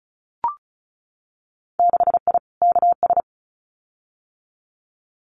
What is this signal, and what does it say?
Morse code, heard as a keyed tone, about 34 wpm, 700 Hz. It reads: 6I KS
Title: Winter Field Day Exchanges Encore 35wpm